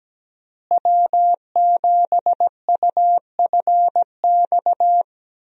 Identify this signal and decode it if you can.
Morse code, heard as a keyed tone, about 17 wpm, 700 Hz. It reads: W7UFX
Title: USA Call Signs - 5 Characters - Repeated 2x - 17wpm